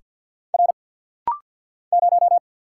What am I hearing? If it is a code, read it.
Morse code, heard as a keyed tone, about 50 wpm, 700 Hz. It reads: R 0
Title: Single Letter-Number Alt - Rapid-Fire 50wpm